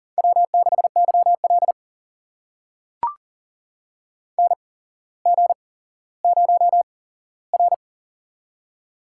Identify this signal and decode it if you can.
Morse code, heard as a keyed tone, about 38 wpm, spaced as for 17 wpm, 700 Hz. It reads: W6YL NG0R
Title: USA Call Signs - 4 Characters - 8x Character Spacing - 40wpm